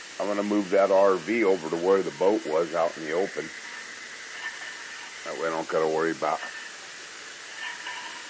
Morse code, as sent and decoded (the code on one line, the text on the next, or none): none